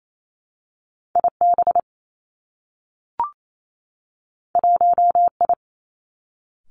Morse code, heard as a keyed tone, about 28 wpm, 700 Hz. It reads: I B 1 I
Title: Two Challenging Letter-Number Combinations 28wpm